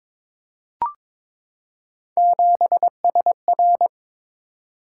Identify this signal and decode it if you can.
Morse code, heard as a keyed tone, about 22 wpm, 700 Hz. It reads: 7SR
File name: practice / 22wpm / Three Letter-Number Combinations - 22wpm